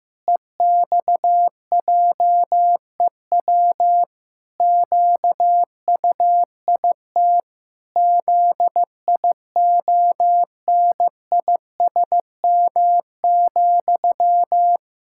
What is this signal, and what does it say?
Morse code, heard as a keyed tone, about 15 wpm, 700 Hz. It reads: EXJEW QUIT ZIONISM,
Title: Pangram Challenge v2 - 15wpm